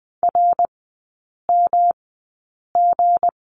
Morse code, heard as a keyed tone, about 20 wpm, spaced as for 8 wpm, 690 Hz. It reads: RMG
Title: 159 Lesson 26e - Letter g - Three Characters 20wpm